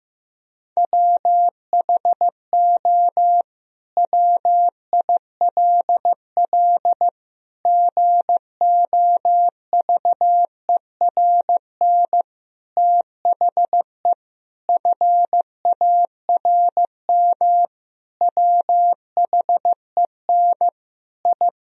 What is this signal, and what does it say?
Morse code, heard as a keyed tone, about 15 wpm, 700 Hz. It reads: WHO WILL GOVERN THE FARM WHEN I
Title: Sentences from Top 1000 Words - Review Part 2 - 15wpm